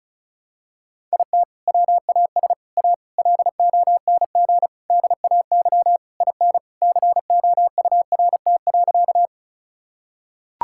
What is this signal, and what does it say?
Morse code, heard as a keyed tone, about 34 wpm, 700 Hz. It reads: IT WAS A LONG DAY IN COURT.
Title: Sentences from Top 1600 Words 35wpm